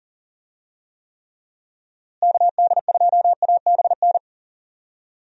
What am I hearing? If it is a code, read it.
Morse code, heard as a keyed tone, about 40 wpm, 700 Hz. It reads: KD2ABN
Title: USA Call Signs - 6 Characters - Repeated 2x - 40wpm